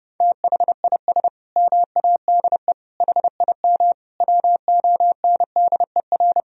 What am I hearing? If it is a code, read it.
Morse code, heard as a keyed tone, about 30 wpm, 700 Hz. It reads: THIS MADE HIM WONDER
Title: Sentences from Top 500 Words and Abbreviations 30wpm